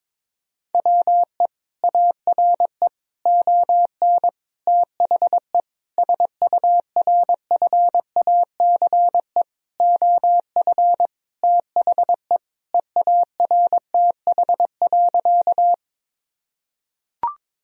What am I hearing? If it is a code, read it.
Morse code, heard as a keyed tone, about 22 wpm, 700 Hz. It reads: WE ARE ON THE SURFACE OF THE EARTH.
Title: Sentences from Top 1000 Words - Review Part 3 - 22wpm